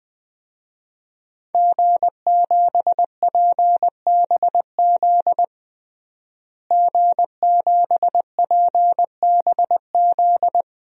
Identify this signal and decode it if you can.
Morse code, heard as a keyed tone, about 20 wpm, 700 Hz. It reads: G7PBZ G7PBZ
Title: DXCC-100 Call Signs - 5 Characters - Repeated 3x - 20wpm